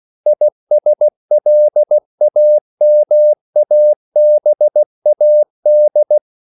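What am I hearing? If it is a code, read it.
Morse code, heard as a keyed tone, about 16 wpm, 600 Hz. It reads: ISLAMABAD
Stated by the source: CW practice of national capitals at 16 words per minute, from ON6ZQ.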